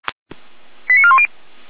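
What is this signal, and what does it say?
Five-tone selective call (ZVEI), heard as quick runs of five tones, tones of about 70 ms. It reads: 89410
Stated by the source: ZVEI five-tone call received by SDR.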